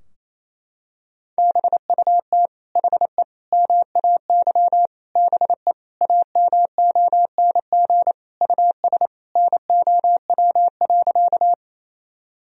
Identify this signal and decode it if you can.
Morse code, heard as a keyed tone, about 27 wpm, 700 Hz. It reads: BUT HE MAY BE AMONG US NOW.
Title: Sentences from Top 1000 Words - Review Part 3 - 28wpm